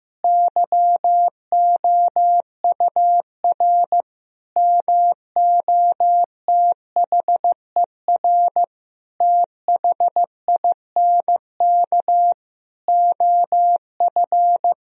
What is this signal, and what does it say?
Morse code, heard as a keyed tone, about 16 wpm, 700 Hz. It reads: YOUR MOTHER THINK OF
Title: Sentences from Top 1000 Words - Review Part 4 - 15wpm